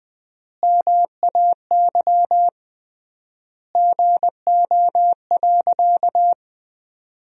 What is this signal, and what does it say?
Morse code, heard as a keyed tone, about 20 wpm, 700 Hz. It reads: MAY GO.
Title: 3x Word-Spacing - Sentences from Top 100 Words - 20wpm